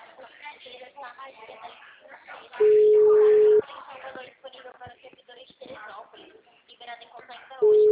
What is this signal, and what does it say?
Ringback tone (Europe, 425 Hz), with speech.